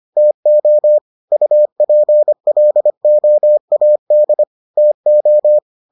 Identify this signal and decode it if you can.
Morse code, heard as a keyed tone, about 25 wpm, 600 Hz. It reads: TO UPLOAD TO